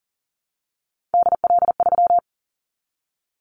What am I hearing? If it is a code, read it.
Morse code, heard as a keyed tone, about 39 wpm, 700 Hz. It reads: DL3
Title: Callsign Prefixes for Top 25 DXCC - Repeated 3x - 40wpm